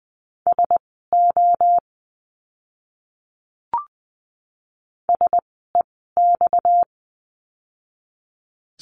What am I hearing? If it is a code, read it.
Morse code, heard as a keyed tone, about 20 wpm, spaced as for 14 wpm, 700 Hz. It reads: SO SEX
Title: 230 Lesson 37f - Letter x - Words 20wpm